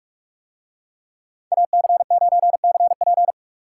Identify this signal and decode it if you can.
Morse code, heard as a keyed tone, about 46 wpm, 700 Hz. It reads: AC9CP